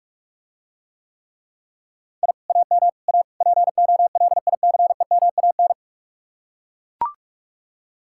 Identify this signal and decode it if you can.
Morse code, heard as a keyed tone, about 49 wpm, 700 Hz. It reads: I AM A POLICEMAN